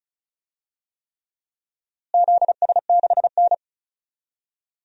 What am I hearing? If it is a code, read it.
Morse code, heard as a keyed tone, about 36 wpm, 700 Hz. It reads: ZS6N
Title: DXCC-100 Call Signs - 4 Characters - Repeated 3x - 35wpm